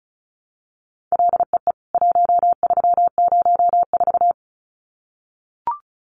Morse code, heard as a keyed tone, about 34 wpm, 700 Hz. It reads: LEE 1304
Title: CWT Exchange 35wpm